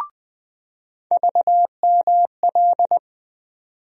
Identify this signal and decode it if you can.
Morse code, heard as a keyed tone, about 20 wpm, 700 Hz. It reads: VML
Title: USA Call Sign Postfixes - 3 Letters 20wpm